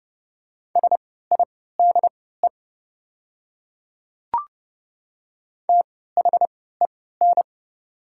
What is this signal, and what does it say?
Morse code, heard as a keyed tone, about 30 wpm, spaced as for 17 wpm, 700 Hz. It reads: SIDE THEN
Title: 3x Character Spacing - Top 100 Words - 30wpm